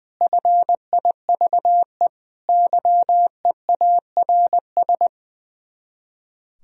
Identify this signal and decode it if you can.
Morse code, heard as a keyed tone, about 20 wpm, 700 Hz. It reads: FIVE YEARS
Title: Speed-Racing - Sets of 5 Words 20wpm